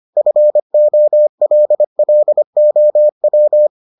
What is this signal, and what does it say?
Morse code, heard as a keyed tone, about 25 wpm, 600 Hz. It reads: FOLLOW